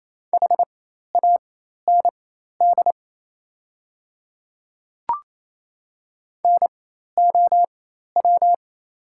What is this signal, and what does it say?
Morse code, heard as a keyed tone, about 27 wpm, spaced as for 13 wpm, 700 Hz. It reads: HAND NOW
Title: 4x Character Spacing - Top 200 Words - 28wpm